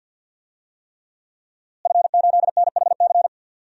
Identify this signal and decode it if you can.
Morse code, heard as a keyed tone, about 50 wpm, 700 Hz. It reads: F8NHC
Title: DXCC-100 Call Signs - 5 Characters - Repeated 3x - 50wpm